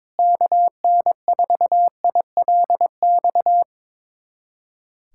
Morse code, heard as a keyed tone, about 21 wpm, 700 Hz. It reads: KN4ILX
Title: USA Callsign DE Callsign - 6 Characters - 22wpm